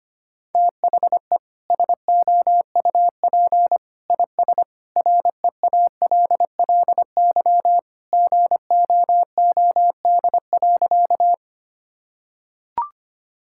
Morse code, heard as a keyed tone, about 25 wpm, 700 Hz. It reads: THE SOUP IS REALLY GOOD.
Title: Sentences from Top 1000 Words - Review Part 6 - 25wpm